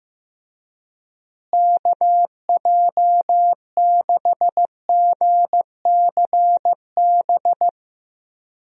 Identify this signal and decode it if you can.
Morse code, heard as a keyed tone, about 15 wpm, 700 Hz. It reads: KJ6GCB
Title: USA Call Signs - 6 Characters - Repeated 2x - 15wpm